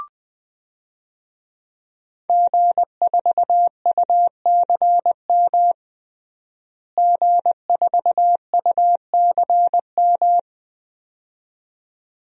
Morse code, heard as a keyed tone, about 20 wpm, 700 Hz. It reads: G4UCM G4UCM
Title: DXCC-100 Call Signs - 5 Characters - Repeated 2x - 20wpm